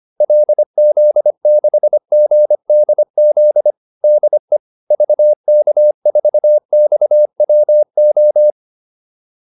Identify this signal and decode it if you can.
Morse code, heard as a keyed tone, about 25 wpm, 600 Hz. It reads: LZ6GDZ DE VK4XWO